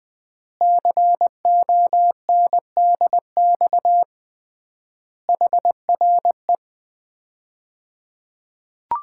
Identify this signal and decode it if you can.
Morse code, heard as a keyed tone, about 20 wpm, 700 Hz. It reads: CONDX HRE
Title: Common QSO Phrases - 3x Word-Spacing - 20wpm